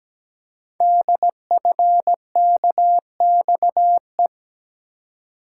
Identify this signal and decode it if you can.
Morse code, heard as a keyed tone, about 17 wpm, 700 Hz. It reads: DFKXE
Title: Code Groups - 5 Letters - 17wpm